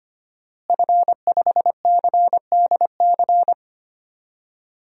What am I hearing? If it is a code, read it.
Morse code, heard as a keyed tone, about 25 wpm, 700 Hz. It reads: F5CDC
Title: DXCC-100 Call Signs - 5 Characters - Repeated 2x - 25wpm